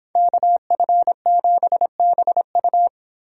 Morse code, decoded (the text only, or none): KF7BU